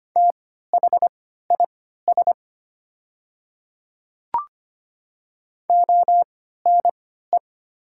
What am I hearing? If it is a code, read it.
Morse code, heard as a keyed tone, about 25 wpm, spaced as for 14 wpm, 700 Hz. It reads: THIS ONE
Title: 3x Character Spacing - Top 100 Words - 25wpm